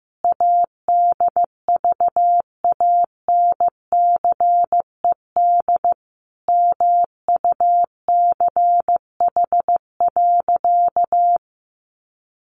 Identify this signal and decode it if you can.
Morse code, heard as a keyed tone, about 15 wpm, 700 Hz. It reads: ADVANCED MUCH.
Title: Sentences from Top 1200 Words 15wpm